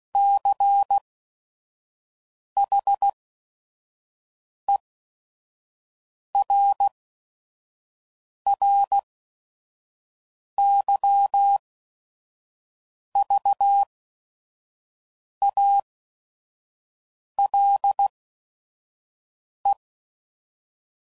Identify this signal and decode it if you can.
Morse code, heard as a keyed tone, about 16 wpm, spaced as for 5 wpm, 800 Hz. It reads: CHERRYVALE
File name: CW Code Groups @5WPM / fromSM's / Cities